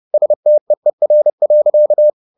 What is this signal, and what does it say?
Morse code, heard as a keyed tone, about 30 wpm, 600 Hz. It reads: STEER.